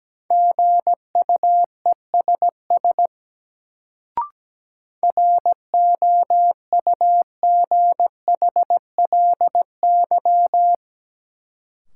Morse code, heard as a keyed tone, about 17 wpm, 700 Hz. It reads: GUESS ROUGHLY
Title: Sentences from Top 1500 Words - New Word Drill - 17wpm